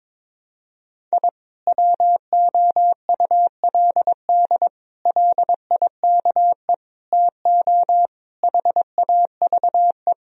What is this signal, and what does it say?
Morse code, heard as a keyed tone, about 22 wpm, 700 Hz. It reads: I WOULD LIKE TO HAVE
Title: Sentences from Top 1000 Words - Review Part 3 - 22wpm